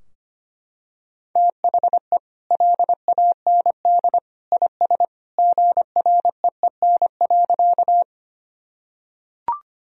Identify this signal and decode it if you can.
Morse code, heard as a keyed tone, about 25 wpm, 700 Hz. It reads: THE LAND IS GREEN.
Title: Sentences from Top 1000 Words - Review Part 2 - 25wpm